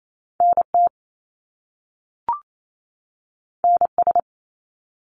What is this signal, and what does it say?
Morse code, heard as a keyed tone, about 28 wpm, 700 Hz. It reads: NT NS
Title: Canadian Provinces and Territories 28wpm